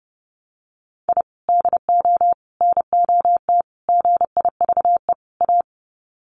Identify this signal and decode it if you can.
Morse code, heard as a keyed tone, about 30 wpm, 700 Hz. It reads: I DO NOT GIVE A
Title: Sets of 5 Words Encore2 30wpm